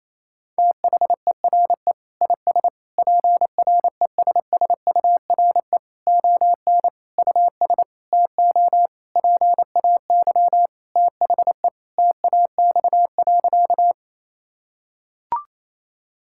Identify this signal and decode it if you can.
Morse code, heard as a keyed tone, about 28 wpm, 700 Hz. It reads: THERE IS PRESSURE ON US TO PAY THE TAX.
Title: Sentences from Top 1000 Words - Review Part 6 - 28wpm